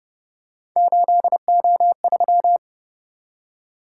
Morse code, heard as a keyed tone, about 31 wpm, 700 Hz. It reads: 8O3
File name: Speed-Racing - Three Letter-Number Combinations - 25wpm